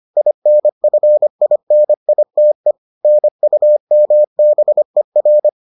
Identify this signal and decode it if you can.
Morse code, heard as a keyed tone, about 25 wpm, 600 Hz. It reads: INFINITE NUMBER